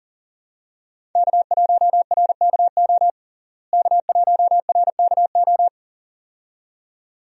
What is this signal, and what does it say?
Morse code, heard as a keyed tone, about 40 wpm, 700 Hz. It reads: K1RKO K1RKO
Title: USA Call Signs - 5 Characters - Repeated 2x - 40wpm